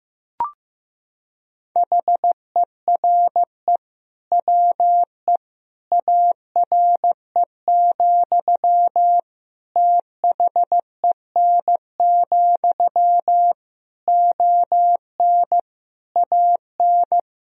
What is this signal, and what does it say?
Morse code, heard as a keyed tone, about 15 wpm, 700 Hz. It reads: HERE WE ARE, THEN, ON AN
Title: Sentences from Top 1000 Words - Review Part 3 - 15wpm